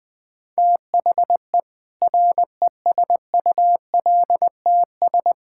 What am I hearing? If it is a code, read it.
Morse code, heard as a keyed tone, about 21 wpm, 700 Hz. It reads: THE RESULTS
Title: Build A Phrase - 5-Word Phrases - 20wpm